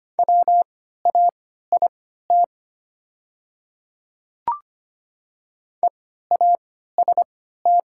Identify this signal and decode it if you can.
Morse code, heard as a keyed tone, about 25 wpm, spaced as for 14 wpm, 700 Hz. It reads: WAIT EAST
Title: 3x Character Spacing - Top 500 Words - 25wpm